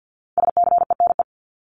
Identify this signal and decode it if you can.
Morse code, heard as a keyed tone, about 50 wpm, 700 Hz. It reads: SCENE
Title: Top 1001-1250 Words 50wpm